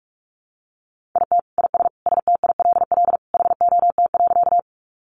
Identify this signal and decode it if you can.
Morse code, heard as a keyed tone, about 44 wpm, 700 Hz. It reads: IT IS STILL HOT.
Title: Additional Easy Phrases 45wpm